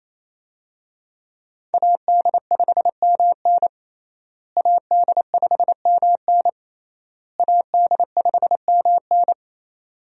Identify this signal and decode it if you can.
Morse code, heard as a keyed tone, about 28 wpm, 700 Hz. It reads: AD5MN AD5MN AD5MN
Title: USA Call Signs - 5 Characters - Repeated 3x - 28wpm